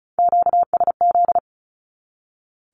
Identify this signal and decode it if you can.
Morse code, heard as a keyed tone, about 34 wpm, 700 Hz. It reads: QSZ
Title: Q-Codes 35wpm